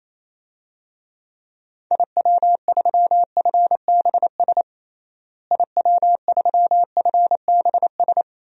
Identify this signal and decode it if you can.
Morse code, heard as a keyed tone, about 28 wpm, 700 Hz. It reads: IW3FBS IW3FBS
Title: DXCC-100 Call Signs - 6 Characters - Repeated 2x - 28wpm